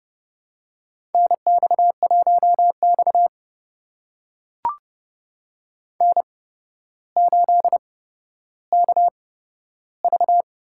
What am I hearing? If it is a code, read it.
Morse code, heard as a keyed tone, about 30 wpm, spaced as for 13 wpm, 700 Hz. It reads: NX1X N8KV
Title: USA Call Signs - 4 Characters - 8x Character Spacing - 30wpm